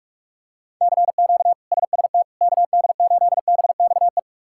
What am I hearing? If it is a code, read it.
Morse code, heard as a keyed tone, about 41 wpm, 700 Hz. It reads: CQ SST KD8BXE
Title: K1USN SST QSO Practice - 45wpm